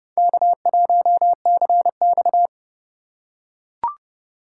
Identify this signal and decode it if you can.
Morse code, heard as a keyed tone, about 32 wpm, 700 Hz. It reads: K1CX
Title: K1USN SST QSO Practice - 30wpm